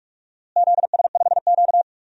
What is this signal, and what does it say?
Morse code, heard as a keyed tone, about 46 wpm, 700 Hz. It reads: ZS5Q
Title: DXCC-100 Call Signs - 4 Characters - Repeated 2x - 45wpm